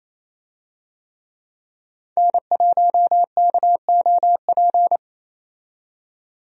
Morse code, heard as a keyed tone, about 28 wpm, 700 Hz. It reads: N1KOP
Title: USA Call Signs - 5 Characters - 28wpm